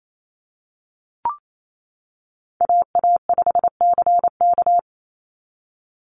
Morse code, heard as a keyed tone, about 28 wpm, 700 Hz. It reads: AA5CK
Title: NA SOTA Chaser Callsigns 28wpm